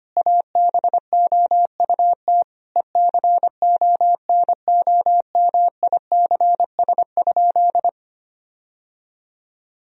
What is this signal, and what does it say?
Morse code, heard as a keyed tone, about 25 wpm, 700 Hz. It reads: ABOUT ECONOMICS?
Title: Sentences from Top 1000 Words - Review Part 3 - 25wpm